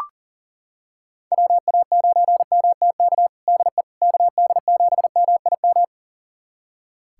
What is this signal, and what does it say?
Morse code, heard as a keyed tone, about 40 wpm, 700 Hz. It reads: WA9MTK DE KD7MIM
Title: USA Callsign DE Callsign - 6 Characters - 40wpm